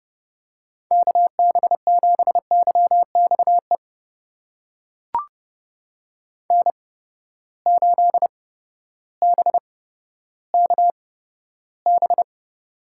Morse code, heard as a keyed tone, about 31 wpm, spaced as for 14 wpm, 700 Hz. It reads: KB7YXE N8BKB
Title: USA Call Signs - 8x Character Spacing - 30wpm